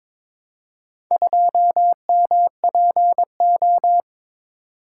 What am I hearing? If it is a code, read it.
Morse code, heard as a keyed tone, about 21 wpm, 700 Hz. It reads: 2MPO
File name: practice / 22wpm / Four Letter-Number Combinations - 22wpm